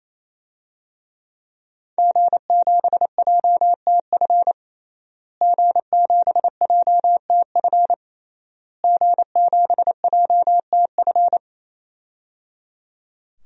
Morse code, heard as a keyed tone, about 27 wpm, 700 Hz. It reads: G7JTF G7JTF G7JTF
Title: DXCC-100 Call Signs - 5 Characters - Repeated 3x - 28wpm